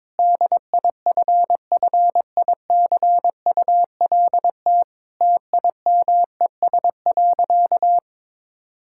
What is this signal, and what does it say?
Morse code, heard as a keyed tone, about 22 wpm, 700 Hz. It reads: DIFFICULT TIMES.
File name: Difficult Sentences with Hints - 22wpm